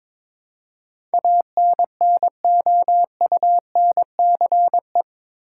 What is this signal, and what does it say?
Morse code, heard as a keyed tone, about 21 wpm, 700 Hz. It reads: ANNOUNCE
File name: practice / 22wpm / Eight Letter Words - 22wpm